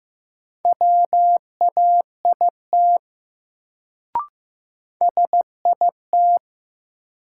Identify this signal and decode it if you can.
Morse code, heard as a keyed tone, about 15 wpm, 700 Hz. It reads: WAIT SIT
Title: Sentences from Top 500 Words - New Word Drill - 15wpm